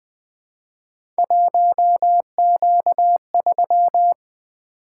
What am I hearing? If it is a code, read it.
Morse code, heard as a keyed tone, about 20 wpm, 700 Hz. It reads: 1Q3
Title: Three Letter-Number Combinations - 20wpm